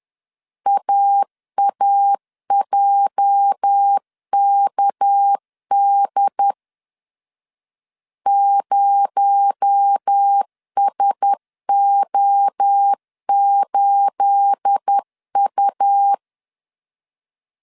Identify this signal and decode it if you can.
Morse code, heard as a keyed tone, about 10 wpm, 800 Hz. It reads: AAJKD 0SO8U
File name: CW / DET / 1 9 1500C S B 10wpm 4439